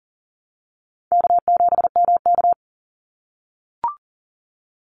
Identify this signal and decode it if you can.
Morse code, heard as a keyed tone, about 40 wpm, 700 Hz. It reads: K7MK
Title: NA SOTA Activator Callsigns 40wpm